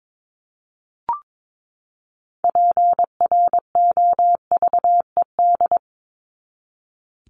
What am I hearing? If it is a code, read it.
Morse code, heard as a keyed tone, about 22 wpm, 700 Hz. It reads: PROVED
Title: ED Suffix - Top 100 Words with Base Words 22wpm